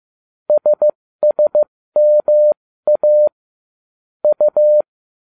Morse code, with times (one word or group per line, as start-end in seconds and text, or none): 0.5-3.3 SSMA
4.2-4.8 U